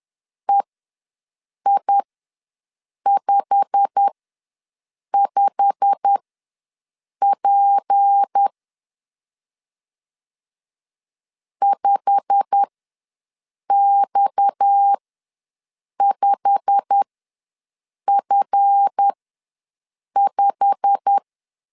Morse code, heard as a keyed tone, about 10 wpm, spaced as for 6 wpm, 800 Hz. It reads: EI55P 5X5F5
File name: LEZIONI / 8 YJ45 / 8 YJ45 ws 10wpm 3843